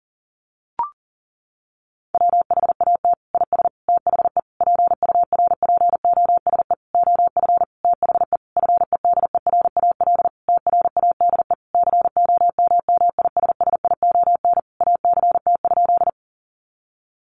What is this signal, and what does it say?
Morse code, heard as a keyed tone, about 40 wpm, 700 Hz. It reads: WHAT IS THE PURPOSE OF THE FEDERAL TRADE COMMISSION ACT?
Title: Sentences from Top 1400 Words 40wpm